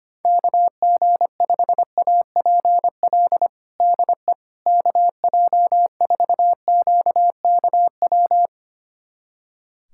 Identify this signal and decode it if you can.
Morse code, heard as a keyed tone, about 25 wpm, 700 Hz. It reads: KG5APL DE KJ4QKW
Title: USA Callsign DE Callsign 25wpm